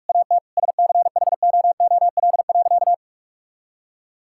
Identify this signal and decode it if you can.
Morse code, heard as a keyed tone, about 41 wpm, 700 Hz. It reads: AT SCHOOL.
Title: Sentences from Top 1000 Words - Review Part 6 - 45wpm